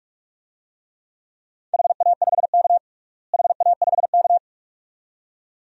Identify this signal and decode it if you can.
Morse code, heard as a keyed tone, about 42 wpm, 700 Hz. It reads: HA5K HA5K
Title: DXCC-100 Call Signs - 4 Characters - Repeated 2x - 45wpm